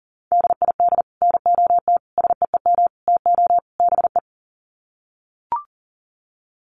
Morse code, heard as a keyed tone, about 37 wpm, 700 Hz. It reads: DID NOT SEEM TO BE
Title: Sets of 5 Words Encore 40wpm